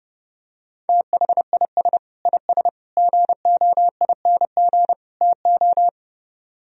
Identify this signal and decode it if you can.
Morse code, heard as a keyed tone, about 29 wpm, 700 Hz. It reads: THIS IS GOING TO